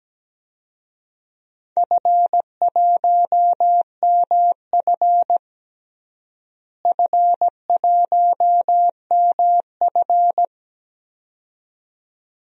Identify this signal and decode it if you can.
Morse code, heard as a keyed tone, about 17 wpm, 700 Hz. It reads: F1MF F1MF
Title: DXCC-100 Call Signs - 4 Characters - Repeated 2x - 17wpm